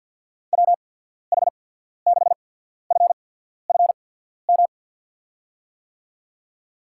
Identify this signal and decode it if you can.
Morse code, heard as a keyed tone, about 53 wpm, spaced as for 14 wpm, 700 Hz. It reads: WH6FFM
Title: USA Call Signs - 6 Characters - 8x Character Spacing - 50wpm